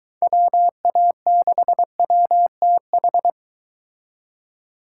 Morse code, heard as a keyed tone, about 22 wpm, 700 Hz. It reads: WA6WTH